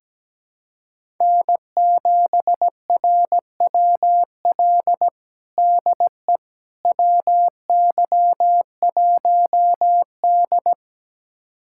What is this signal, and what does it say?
Morse code, heard as a keyed tone, about 17 wpm, 700 Hz. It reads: N7RWL DE WY1D
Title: USA Callsign DE Callsign 17wpm